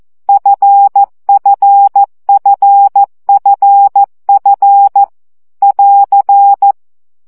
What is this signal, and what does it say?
Morse code, heard as a keyed tone, about 14 wpm, 800 Hz. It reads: FFFFF <AR>
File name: CW gyakorlas f 12 baud